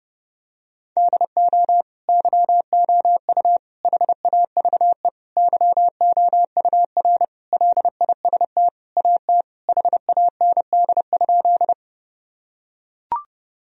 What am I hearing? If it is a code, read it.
Morse code, heard as a keyed tone, about 32 wpm, 700 Hz. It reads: DO YOU HAVE YOUR LIST AT HAND?
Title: Sentences from Top 1000 Words - Review Part 2 - 30wpm